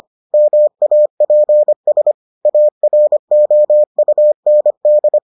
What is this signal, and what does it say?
Morse code, heard as a keyed tone, about 25 wpm, 600 Hz. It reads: MAPS AROUND